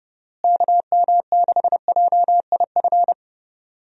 Morse code, heard as a keyed tone, about 30 wpm, 700 Hz. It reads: KM6JIF
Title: USA Call Signs - 6 Characters - Repeated 2x - 30wpm